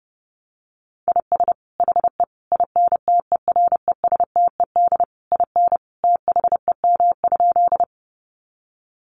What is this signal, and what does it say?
Morse code, heard as a keyed tone, about 30 wpm, 700 Hz. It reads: IS HE INTERESTED IN THEM?
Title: Sentences from Top 900 Words 30wpm